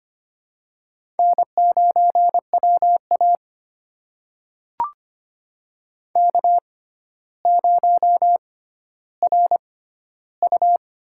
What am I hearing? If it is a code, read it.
Morse code, heard as a keyed tone, about 25 wpm, spaced as for 13 wpm, 700 Hz. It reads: N9WA K0RU
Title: USA Call Signs - 4 Characters - 6x Character Spacing - 25wpm